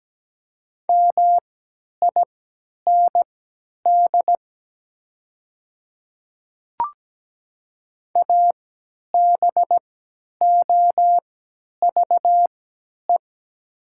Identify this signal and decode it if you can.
Morse code, heard as a keyed tone, about 17 wpm, spaced as for 10 wpm, 700 Hz. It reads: MIND ABOVE
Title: 3x Character Spacing - Top 500 Words - 17wpm